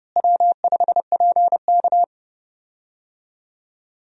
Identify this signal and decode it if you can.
Morse code, heard as a keyed tone, about 30 wpm, 700 Hz. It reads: W5PK
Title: USA Call Signs - 4 Characters - Repeated 3x - 30wpm